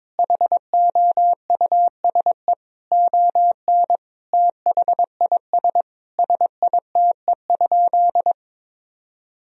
Morse code, heard as a keyed tone, about 22 wpm, 700 Hz. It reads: HOUSE ON THIS SITE?
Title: Sentences from Top 1000 Words - Review Part 4 - 22wpm